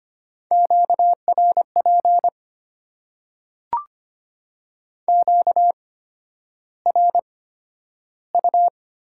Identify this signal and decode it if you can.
Morse code, heard as a keyed tone, about 25 wpm, spaced as for 11 wpm, 700 Hz. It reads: QRP QRU
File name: Top 90 QSO Elements - 8x Character Spacing - 25wpm